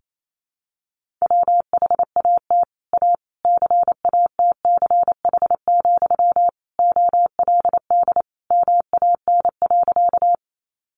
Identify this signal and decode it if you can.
Morse code, heard as a keyed tone, about 28 wpm, 700 Hz. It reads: WHAT A CATCH, OLD MAN.
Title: Sentences from Top 1500 Words 28wpm